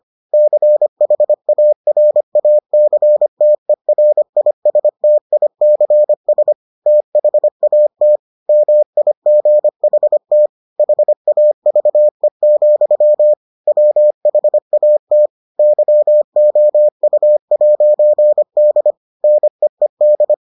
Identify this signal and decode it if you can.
Morse code, heard as a keyed tone, about 25 wpm, 600 Hz. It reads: CHARACTERISTICS THAT MIGHT HAVE, WHAT YOU'D NEED